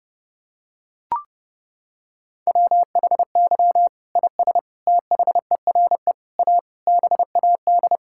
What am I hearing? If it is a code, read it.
Morse code, heard as a keyed tone, about 30 wpm, 700 Hz. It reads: WHY IS THERE A BAD
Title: Sentences from Top 1600 Words 30wpm